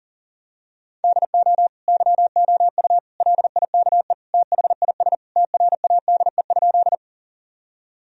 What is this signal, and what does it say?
Morse code, heard as a keyed tone, about 37 wpm, 700 Hz. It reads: DO YOU LIKE THIS TRADE?
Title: Sentences from Top 1000 Words - Review Part 4 - 40wpm